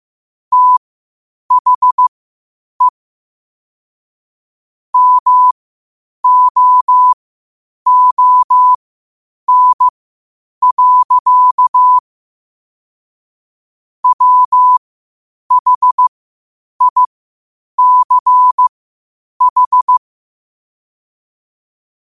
Morse code, heard as a keyed tone, about 15 wpm, spaced as for 8 wpm, 1000 Hz. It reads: THE MOON. WHICH
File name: qotdfiles / 8wpm